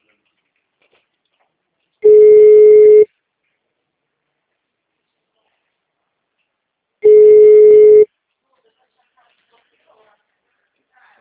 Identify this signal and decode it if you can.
Ringback tone (Europe, 425 Hz).
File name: out- -2004- - - .81